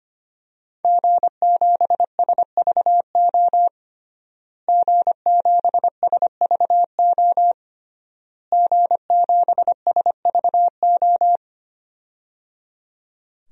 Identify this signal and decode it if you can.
Morse code, heard as a keyed tone, about 25 wpm, 700 Hz. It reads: G7SVO G7SVO G7SVO
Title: DXCC-100 Call Signs - 5 Characters - Repeated 3x - 25wpm